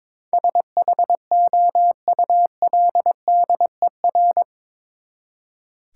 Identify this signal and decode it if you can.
Morse code, heard as a keyed tone, about 22 wpm, 700 Hz. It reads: SHOULDER
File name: practice / 22wpm / Eight Letter Words - 22wpm